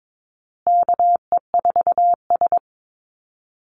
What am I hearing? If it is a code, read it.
Morse code, heard as a keyed tone, about 22 wpm, 690 Hz. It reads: KE4S
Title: Call Signs Repeated 3x - 22wpm